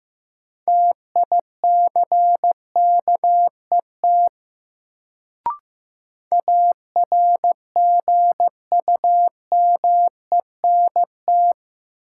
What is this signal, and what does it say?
Morse code, heard as a keyed tone, about 15 wpm, 700 Hz. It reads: TICKET ARGUMENT
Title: Sentences from Top 1400 Words - New Word Drill - 15wpm